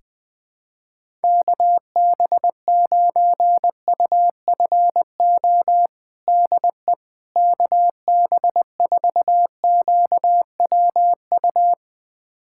Morse code, heard as a keyed tone, about 20 wpm, 700 Hz. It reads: KB9UFO DE KB4QWU
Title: USA Callsign DE Callsign - 6 Characters - 20wpm